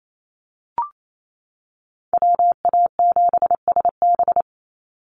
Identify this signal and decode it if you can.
Morse code, heard as a keyed tone, about 28 wpm, 700 Hz. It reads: WA7SB
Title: NA SOTA Chaser Callsigns 28wpm